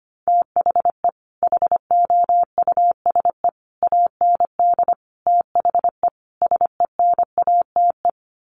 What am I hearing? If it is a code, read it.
Morse code, heard as a keyed tone, about 25 wpm, 700 Hz. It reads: THE HOUSE AND THE SENATE
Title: Sets of 5 Words Encore2 25wpm